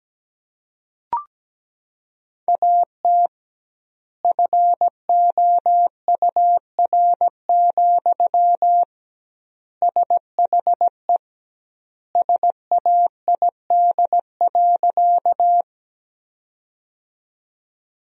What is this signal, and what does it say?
Morse code, heard as a keyed tone, about 17 wpm, 700 Hz. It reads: AT FOUR, SHE SAID.
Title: 2x Word-Spacing - Sentences from Top 300 Words - 17wpm